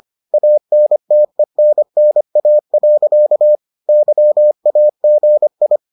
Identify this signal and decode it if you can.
Morse code, heard as a keyed tone, about 25 wpm, 600 Hz. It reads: ANTENNA. YAGI